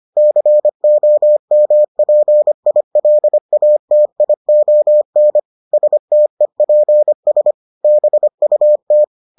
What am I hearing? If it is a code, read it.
Morse code, heard as a keyed tone, about 25 wpm, 600 Hz. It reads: COMPILATION STEPS BUT